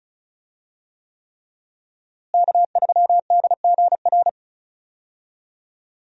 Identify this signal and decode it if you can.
Morse code, heard as a keyed tone, about 36 wpm, 700 Hz. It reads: K3DGR
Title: USA Call Signs - 5 Characters - Repeated 3x - 35wpm